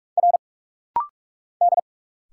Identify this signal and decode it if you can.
Morse code, heard as a keyed tone, about 44 wpm, 700 Hz. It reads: R D
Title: Single Letters - Rapid-Fire 45wpm